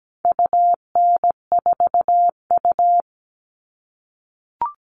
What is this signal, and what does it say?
Morse code, heard as a keyed tone, about 17 wpm, 700 Hz. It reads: UN4U
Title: All Cut Numbers - 4-Digit 17wpm